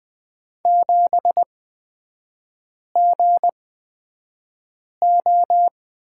Morse code, heard as a keyed tone, about 20 wpm, spaced as for 5 wpm, 700 Hz. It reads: 7GO
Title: 255 Lesson 41e - Period - Three Characters 20wpm